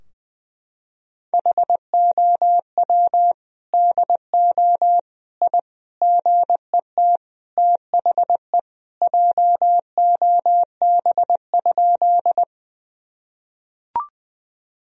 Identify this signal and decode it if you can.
Morse code, heard as a keyed tone, about 20 wpm, 700 Hz. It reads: HOW DO I GET THE JOB?
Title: Sentences from Top 1000 Words - Review Part 3 - 20wpm